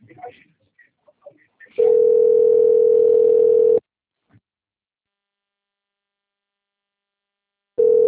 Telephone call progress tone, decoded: ringback tone

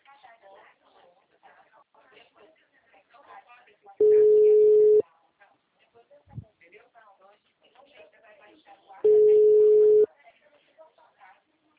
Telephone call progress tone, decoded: ringback tone

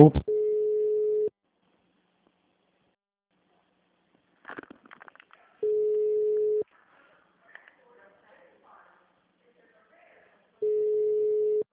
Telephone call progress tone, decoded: ringback tone

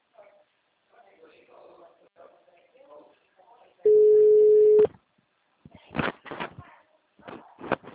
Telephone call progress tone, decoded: ringback tone

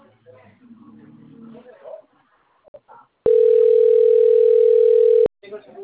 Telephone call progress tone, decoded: ringback tone

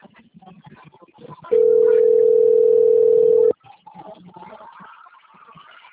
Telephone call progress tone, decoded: ringback tone

